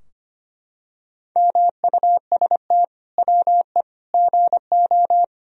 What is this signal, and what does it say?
Morse code, heard as a keyed tone, about 25 wpm, 700 Hz. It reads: MUST WE GO
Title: Sentences from Top 1000 Words - Review Part 3 - 25wpm